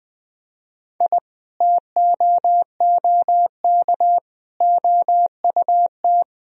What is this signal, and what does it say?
Morse code, heard as a keyed tone, about 20 wpm, 700 Hz. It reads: I TOOK OUT
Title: Sentences from Top 1000 Words - Review Part 2 - 20wpm